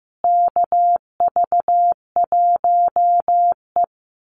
Morse code, heard as a keyed tone, about 15 wpm, 700 Hz. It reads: KV1E
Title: Call Signs Repeated 2x - 15wpm